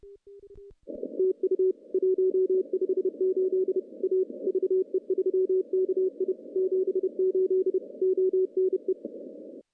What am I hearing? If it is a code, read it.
Morse code, heard as a keyed tone, about 30 wpm, 380 Hz. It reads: TU 158 A VE3KI 78 ONE
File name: contests / CW-SS-k5tr- -0100